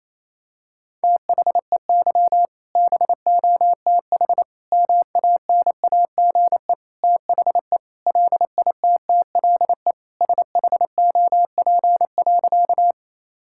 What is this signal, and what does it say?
Morse code, heard as a keyed tone, about 27 wpm, 700 Hz. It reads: THEY BOTH MANAGE THE LITTLE SHOP.
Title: Sentences from Top 1600 Words 28wpm